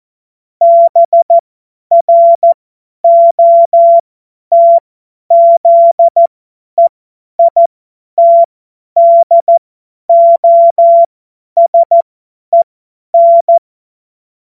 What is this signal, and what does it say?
Morse code, heard as a keyed tone, about 14 wpm, spaced as for 10 wpm, 680 Hz. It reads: BROTZEITDOSEN
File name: Witz-21-Oktober-2025-50-F70